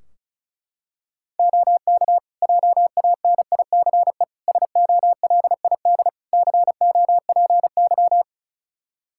Morse code, heard as a keyed tone, about 36 wpm, 700 Hz. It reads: OK JANICE SOLID COPY
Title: K1USN SST QSO Practice 35wpm